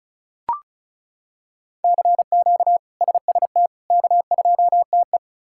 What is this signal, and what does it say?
Morse code, heard as a keyed tone, about 34 wpm, 700 Hz. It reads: CQ SST K2TE
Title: K1USN SST QSO Practice - 35wpm